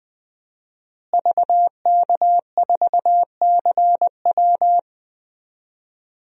Morse code, heard as a keyed tone, about 21 wpm, 700 Hz. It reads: VK4CW
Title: DXCC-100 Call Signs - 5 Characters - Repeated 2x - 20wpm